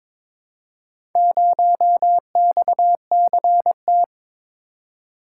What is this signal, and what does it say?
Morse code, heard as a keyed tone, about 22 wpm, 700 Hz. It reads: 0XCT
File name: practice / 22wpm / Four Letter-Number Combinations - 22wpm